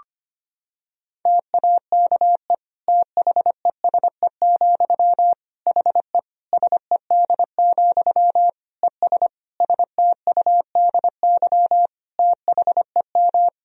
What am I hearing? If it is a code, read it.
Morse code, heard as a keyed tone, about 25 wpm, 700 Hz. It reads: TAKE THESE, HE SED, ES STUDY THEM
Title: Sentences from Top 300 Words and Abbreviations 25wpm